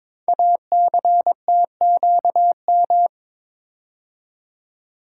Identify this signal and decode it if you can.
Morse code, heard as a keyed tone, about 22 wpm, 700 Hz. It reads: ACTQM
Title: Code Groups - 5 Letters - 22wpm